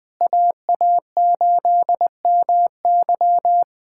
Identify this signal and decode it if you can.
Morse code, heard as a keyed tone, about 20 wpm, 700 Hz. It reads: AA8MY
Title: USA Callsign DE Callsign - 5 Characters - 20wpm